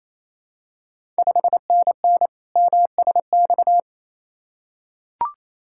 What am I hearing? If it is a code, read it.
Morse code, heard as a keyed tone, about 28 wpm, 700 Hz. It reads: 5NN MSX
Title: Ontario QSO Party Exchange 28wpm